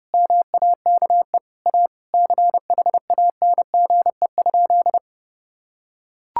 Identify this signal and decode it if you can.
Morse code, heard as a keyed tone, about 30 wpm, 700 Hz. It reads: MAKE A CHANGE?
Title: Sentences from Top 1600 Words 30wpm